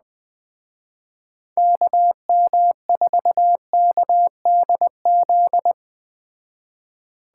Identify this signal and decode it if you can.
Morse code, heard as a keyed tone, about 20 wpm, 700 Hz. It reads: KM4KDZ